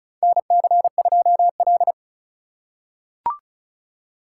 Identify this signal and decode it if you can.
Morse code, heard as a keyed tone, about 36 wpm, 700 Hz. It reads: NC2L
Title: USA Call Signs - 4 Characters - 4x Character Spacing - 35wpm